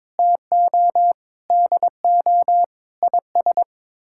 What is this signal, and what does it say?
Morse code, heard as a keyed tone, about 22 wpm, 700 Hz. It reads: TO DO IS
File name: practice / 22wpm / Sets of 6 Words - 22wpm